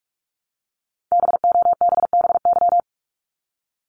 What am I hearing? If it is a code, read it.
Morse code, heard as a keyed tone, about 46 wpm, 700 Hz. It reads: BOBBY
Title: Multi-letter Phonemes 45wpm